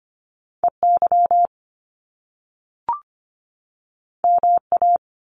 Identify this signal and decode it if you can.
Morse code, heard as a keyed tone, about 25 wpm, 700 Hz. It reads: EY MA
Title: Top Two Letter Combinations in Words 25wpm